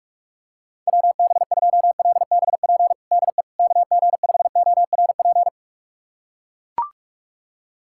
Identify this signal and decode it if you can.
Morse code, heard as a keyed tone, about 41 wpm, 700 Hz. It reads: WB2LBP DE KG5ORP
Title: USA Callsign DE Callsign - 6 Characters - 45wpm